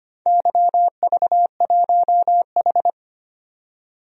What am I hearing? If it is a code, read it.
Morse code, heard as a keyed tone, about 25 wpm, 700 Hz. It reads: YV1H
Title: DXCC-100 Call Signs - 4 Characters - 25wpm